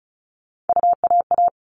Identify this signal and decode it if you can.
Morse code, heard as a keyed tone, about 34 wpm, 700 Hz. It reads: UAA
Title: All Cut Numbers - 3-Digit 35wpm